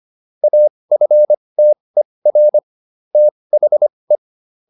Morse code, heard as a keyed tone, about 25 wpm, spaced as for 20 wpm, 600 Hz. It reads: AFTER THE
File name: SS306 20